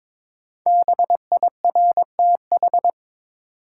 Morse code, heard as a keyed tone, about 22 wpm, 700 Hz. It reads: BIRTH
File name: 5 Letter Words - First 3 Letters Given - 22wpm